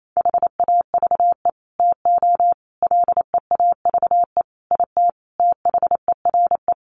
Morse code, heard as a keyed tone, about 28 wpm, 700 Hz. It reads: HAVE TO LEAVE IT THERE
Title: Sets of 5 Words Encore2 28wpm